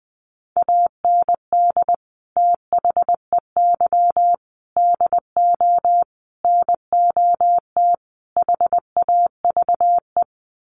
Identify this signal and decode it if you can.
Morse code, heard as a keyed tone, about 20 wpm, 700 Hz. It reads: AND THEY DO NOT HAVE
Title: Sets of 5 Words Encore2 20wpm